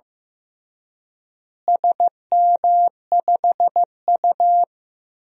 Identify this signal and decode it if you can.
Morse code, heard as a keyed tone, about 16 wpm, 700 Hz. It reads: SM5U
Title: DXCC-100 Call Signs - 4 Characters - Repeated 3x - 15wpm